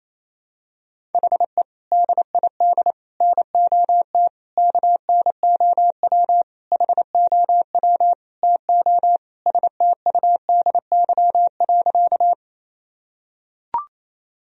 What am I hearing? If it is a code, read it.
Morse code, heard as a keyed tone, about 28 wpm, 700 Hz. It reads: HE DID NOT KNOW HOW TO STUDY.
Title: Sentences from Top 1000 Words - Review Part 1 - 28wpm